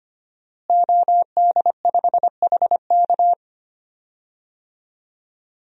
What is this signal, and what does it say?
Morse code, heard as a keyed tone, about 25 wpm, 700 Hz. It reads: OD5HK
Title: DXCC-100 Call Signs - 5 Characters - Repeated 3x - 25wpm